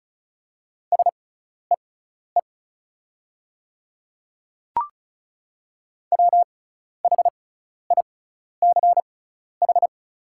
Morse code, heard as a keyed tone, about 35 wpm, spaced as for 12 wpm, 700 Hz. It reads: SEE WHICH
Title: 6x Character Spacing - Top 100 Words - 35wpm